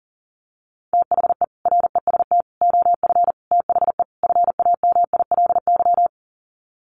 Morse code, heard as a keyed tone, about 40 wpm, 700 Hz. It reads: THE REST OF THE FAMILY
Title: Sets of 5 Words Encore2 40wpm